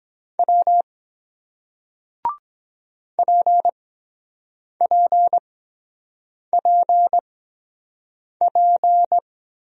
Morse code, heard as a keyed tone, about 23 wpm, spaced as for 7 wpm, 700 Hz. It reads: W PPPP